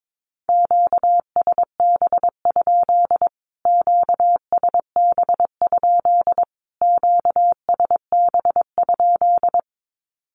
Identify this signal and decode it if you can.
Morse code, heard as a keyed tone, about 22 wpm, 700 Hz. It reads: QSB? QSB? QSB?
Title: Q-Codes Repeated 3x 22wpm